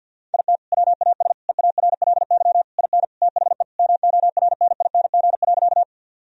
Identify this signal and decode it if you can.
Morse code, heard as a keyed tone, about 50 wpm, 700 Hz. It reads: IT WAS EARLY IN THE MORNING.
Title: Sentences from Top 1000 Words - Review Part 2 - 50wpm